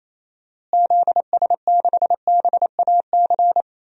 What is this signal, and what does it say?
Morse code, heard as a keyed tone, about 28 wpm, 700 Hz. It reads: ZS6BAC
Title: DXCC-100 Call Signs - 6 Characters - Repeated 3x - 28wpm